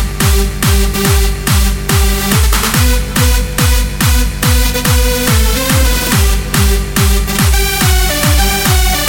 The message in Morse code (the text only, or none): none